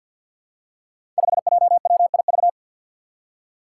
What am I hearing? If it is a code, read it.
Morse code, heard as a keyed tone, about 49 wpm, 700 Hz. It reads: 5JWIV
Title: Five Challenging Letter-Number Combinations 50wpm